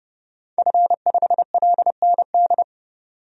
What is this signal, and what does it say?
Morse code, heard as a keyed tone, about 29 wpm, 700 Hz. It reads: F5LND
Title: DXCC-100 Call Signs - 5 Characters - 30wpm